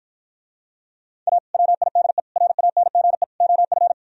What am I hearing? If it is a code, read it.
Morse code, heard as a keyed tone, about 54 wpm, 700 Hz. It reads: A WIDE RANGE OF